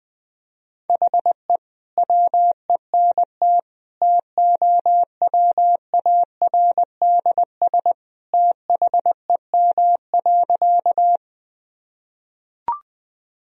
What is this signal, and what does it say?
Morse code, heard as a keyed tone, about 20 wpm, 700 Hz. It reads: HE WENT TOWARDS THEM.
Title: Sentences from Top 1000 Words - Review Part 2 - 20wpm